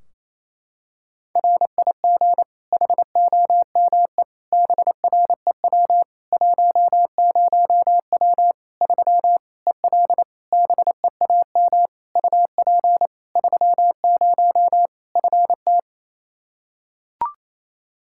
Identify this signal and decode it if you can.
Morse code, heard as a keyed tone, about 27 wpm, 700 Hz. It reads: RIG HOME BREW 10W 3 EL BEAM UP 30 FT
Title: Practice QSO - Exchange 2 - Info - 28wpm